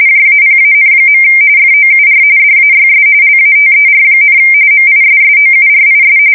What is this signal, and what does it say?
RTTY, about 45.45 baud. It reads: JAPAN. FOUR DAYS LATER, MISSOURIS
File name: fsk170 / USS-Missouri